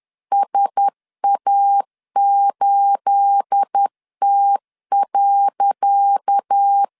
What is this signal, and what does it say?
Morse code, heard as a keyed tone, about 10 wpm, 800 Hz. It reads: SA8T.
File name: LEZIONI / 10 SIGNS / 10C DOT 10wpm 2940